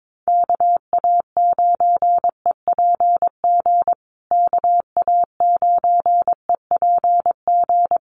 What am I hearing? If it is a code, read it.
Morse code, heard as a keyed tone, about 22 wpm, 700 Hz. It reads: KA9EPG KA9EPG
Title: USA Call Signs Repeated 3x 22wpm